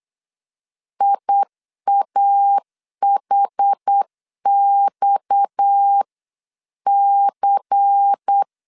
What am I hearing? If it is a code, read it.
Morse code, heard as a keyed tone, about 8 wpm, 800 Hz. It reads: IAHXC
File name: LEZIONI / 7 GF9Z / 7 GF9 8wpm 3545